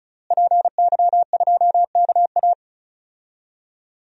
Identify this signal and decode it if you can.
Morse code, heard as a keyed tone, about 36 wpm, 700 Hz. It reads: PY2KA
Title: DXCC-100 Call Signs - 5 Characters - Repeated 2x - 35wpm